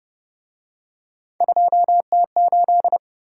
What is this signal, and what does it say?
Morse code, heard as a keyed tone, about 29 wpm, 700 Hz. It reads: 2T8